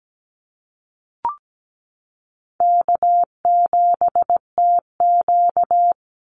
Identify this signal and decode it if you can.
Morse code, heard as a keyed tone, about 17 wpm, 700 Hz. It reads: K7TQ
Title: NA SOTA Chaser Callsigns 17wpm